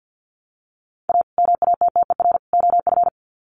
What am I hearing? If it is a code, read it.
Morse code, heard as a keyed tone, about 48 wpm, 700 Hz. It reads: A MATTER OF
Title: Sets of 3 Words 50wpm